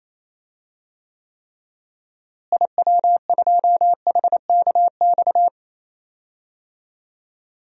Morse code, heard as a keyed tone, about 28 wpm, 700 Hz. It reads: IW2HKX
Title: DXCC-100 Call Signs - 6 Characters - Repeated 3x - 28wpm